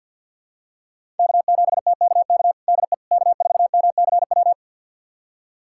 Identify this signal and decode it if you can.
Morse code, heard as a keyed tone, about 52 wpm, 700 Hz. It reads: K7TKK DE K4MCW
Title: USA Callsign DE Callsign - 5 Characters - 50wpm